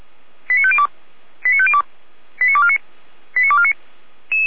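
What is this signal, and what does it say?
Five-tone selective call (ZVEI), heard as quick runs of five tones, tones of about 70 ms. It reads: 89502 89502 89249 89249
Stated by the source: ZVEI five-tone call received on an SDR.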